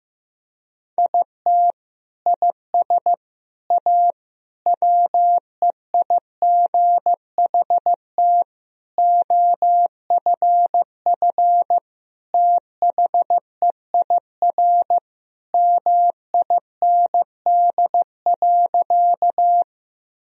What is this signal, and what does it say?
Morse code, heard as a keyed tone, about 15 wpm, 700 Hz. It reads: IT IS A WEIGHT OFF THEIR MIND.
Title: Sentences from Top 1000 Words - Review Part 3 - 15wpm